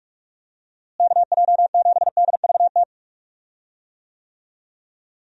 Morse code, heard as a keyed tone, about 46 wpm, 700 Hz. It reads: KJ7DVT
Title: USA Call Signs - 6 Characters - Repeated 3x - 45wpm